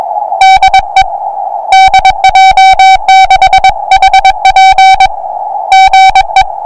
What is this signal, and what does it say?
Morse code, heard as a keyed tone, about 22 wpm, 770 Hz. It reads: DE DJ6HP GE
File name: attachments / 40m up dn